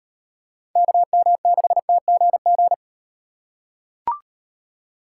Morse code, heard as a keyed tone, about 38 wpm, 700 Hz. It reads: KM6TGG